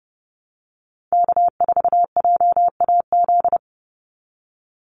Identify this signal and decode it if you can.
Morse code, heard as a keyed tone, about 30 wpm, 700 Hz. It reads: K4JAZ
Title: Call Signs Repeated 3x - 30wpm